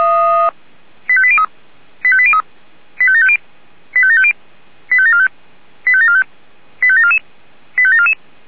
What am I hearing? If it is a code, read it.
Five-tone selective call (ZVEI), heard as quick runs of five tones, tones of about 70 ms. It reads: 85803 85803 85750 85750 85745 85745 85744 85744